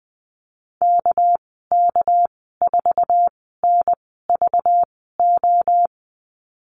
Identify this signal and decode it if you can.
Morse code, heard as a keyed tone, about 20 wpm, spaced as for 15 wpm, 700 Hz. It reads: KK4NVO
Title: 207 Lesson 33g - Letter j - Call Signs 20wpm